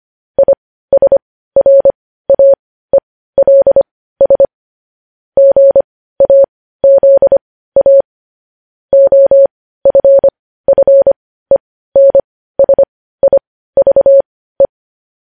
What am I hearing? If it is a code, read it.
Morse code, heard as a keyed tone, about 25 wpm, spaced as for 15 wpm, 550 Hz. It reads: ISRAELS GAZA OFFENSIVE
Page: news-headlines-in-morse-code-at-15-wpm